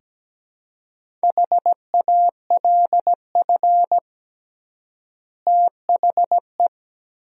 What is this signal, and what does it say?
Morse code, heard as a keyed tone, about 17 wpm, 700 Hz. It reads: HALF THE
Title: 3x Word-Spacing - Sentences from Top 500 Words - 17wpm